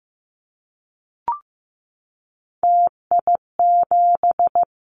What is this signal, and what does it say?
Morse code, heard as a keyed tone, about 15 wpm, 700 Hz. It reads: TI7
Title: Callsign Prefixes for Top 100 DXCC - 15wpm